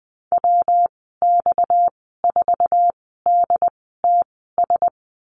Morse code, heard as a keyed tone, about 20 wpm, spaced as for 14 wpm, 700 Hz. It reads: WX4DTS
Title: 231 Lesson 37g - Letter x - Call Signs 20wpm